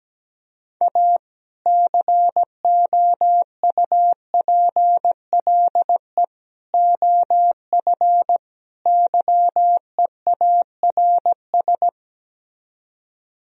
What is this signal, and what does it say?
Morse code, heard as a keyed tone, about 17 wpm, 700 Hz. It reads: A COUPLE OF YEARS
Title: Speed-Racing - Sets of 4 Words 17wpm